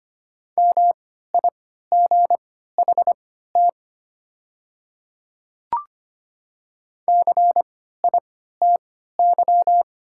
Morse code, heard as a keyed tone, about 25 wpm, spaced as for 14 wpm, 700 Hz. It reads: MIGHT CITY
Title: 3x Character Spacing - Top 300 Words - 25wpm